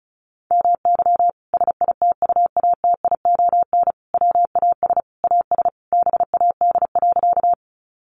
Morse code, heard as a keyed tone, about 34 wpm, 700 Hz. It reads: MY SITUATION WAS AS BAD.
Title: Sentences from Top 1200 Words 35wpm